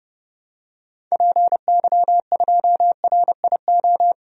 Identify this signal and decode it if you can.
Morse code, heard as a keyed tone, about 32 wpm, 700 Hz. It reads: PY2RIO
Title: DXCC-100 Call Signs - 6 Characters - Repeated 3x - 30wpm